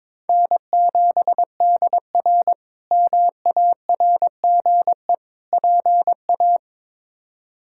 Morse code, heard as a keyed tone, about 22 wpm, 700 Hz. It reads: N7DR MARGE PA
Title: K1USN SST QSO Practice - 22wpm